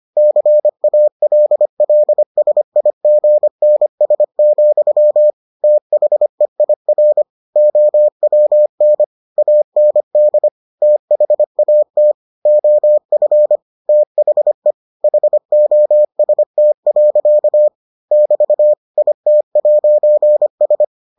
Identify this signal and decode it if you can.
Morse code, heard as a keyed tone, about 25 wpm, 600 Hz. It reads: CALLSIGNS, THEIR OWN AND THAT OF THE HOST. <BT> IT'S